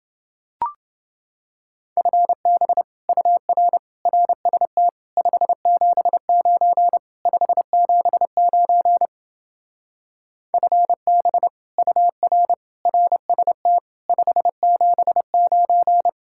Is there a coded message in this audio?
Yes: Morse code from 2.0 to 16.1 s, heard as a keyed tone, about 28 wpm, 700 Hz. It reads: FB UR RST 579 579 FB UR RST 579